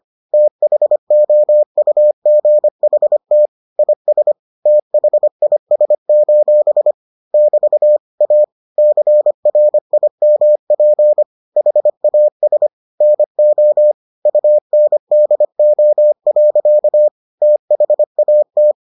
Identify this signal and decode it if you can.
Morse code, heard as a keyed tone, about 25 wpm, 600 Hz. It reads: THOUGHT IS THIS: <BT> A CRIMP HAS NO UNDO. THAT